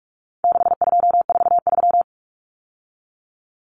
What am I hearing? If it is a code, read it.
Morse code, heard as a keyed tone, about 44 wpm, 700 Hz. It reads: 6243
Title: Numbers - 4-Digit 45wpm